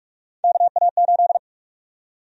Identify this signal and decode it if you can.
Morse code, heard as a keyed tone, about 42 wpm, 700 Hz. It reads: KA8
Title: USA Call Sign Prefixes - 45wpm